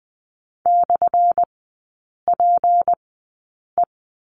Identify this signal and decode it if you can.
Morse code, heard as a keyed tone, about 20 wpm, spaced as for 8 wpm, 700 Hz. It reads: /PE
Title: 178 Lesson 29e - Slash - Three Characters 20wpm